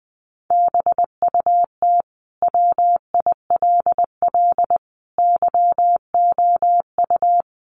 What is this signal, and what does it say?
Morse code, heard as a keyed tone, about 20 wpm, 700 Hz. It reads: BUT WILL YOU
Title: Sentences from Top 1400 Words 20wpm